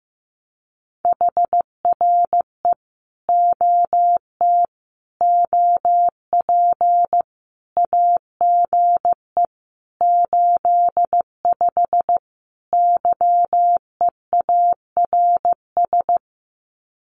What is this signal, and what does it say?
Morse code, heard as a keyed tone, about 15 wpm, 700 Hz. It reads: HRE OT OP AGE 85 YEARS
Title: Common QSO Phrases 15wpm